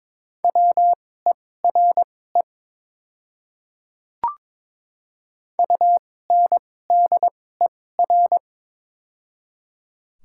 Morse code, heard as a keyed tone, about 21 wpm, spaced as for 16 wpm, 700 Hz. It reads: WERE UNDER
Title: 2x Character Spacing - Top 200 Words - 22wpm